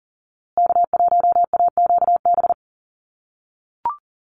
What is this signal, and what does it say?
Morse code, heard as a keyed tone, about 40 wpm, 700 Hz. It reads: K1AQB
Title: USA Call Signs Repeated 3x 40wpm